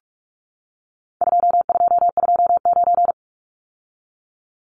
Morse code, heard as a keyed tone, about 44 wpm, 700 Hz. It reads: 2229
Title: Numbers - 4-Digit 45wpm